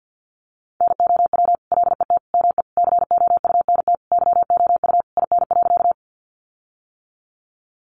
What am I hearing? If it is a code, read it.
Morse code, heard as a keyed tone, about 50 wpm, 700 Hz. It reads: NOW LET ME COUNT YOU IN.
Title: Sentences from Top 1500 Words 50wpm